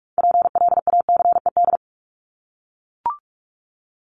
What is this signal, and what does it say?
Morse code, heard as a keyed tone, about 44 wpm, 700 Hz. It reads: PLACED
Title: ED Suffix - Top 100 Words with Base Words 45wpm